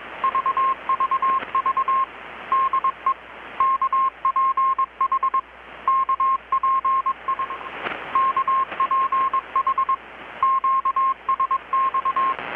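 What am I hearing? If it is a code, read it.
Morse code, heard as a keyed tone, about 21 wpm, 1070 Hz. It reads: VVV DE KPH KPI KPH QSX
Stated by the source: Morse from KPH's Night of Nights XX maritime radiotelegraphy event, received in Chiapas, Mexico about 3,800 km away on a Kenwood R-600 with a grounded 10 m long-wire antenna.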